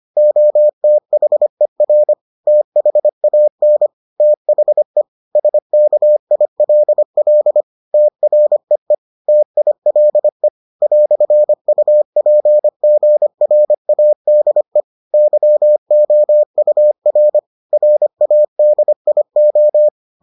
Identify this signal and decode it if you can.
Morse code, heard as a keyed tone, about 25 wpm, 600 Hz. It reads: OTHER THAN THE SKILL TREE TILE "UPGRADE YOUR RADIO